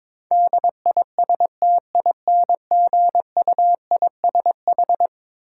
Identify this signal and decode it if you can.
Morse code, heard as a keyed tone, about 22 wpm, 700 Hz. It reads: DISTINGUISH
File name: practice / 22wpm / Eleven Letter Words - 22wpm